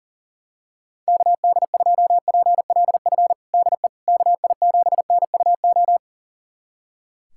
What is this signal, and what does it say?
Morse code, heard as a keyed tone, about 40 wpm, 700 Hz. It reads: KD2PLF DE KI7NUO
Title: USA Callsign DE Callsign - 6 Characters - 40wpm